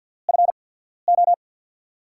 Morse code, heard as a keyed tone, about 51 wpm, spaced as for 14 wpm, 700 Hz. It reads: FO
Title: USA Call Signs - 5 Characters - 8x Character Spacing - 50wpm